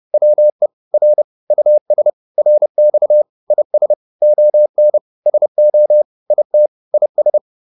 Morse code, heard as a keyed tone, about 30 wpm, 600 Hz. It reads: WE R US RX IS ON SO IT IS